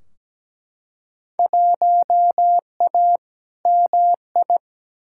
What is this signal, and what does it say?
Morse code, heard as a keyed tone, about 17 wpm, 700 Hz. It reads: 1A MI
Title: Field Day Exchanges - 17wpm